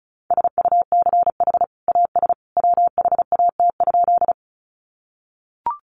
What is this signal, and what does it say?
Morse code, heard as a keyed tone, about 34 wpm, 700 Hz. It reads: SUCH AS WHAT?
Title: Sentences from Top 200 Words 35wpm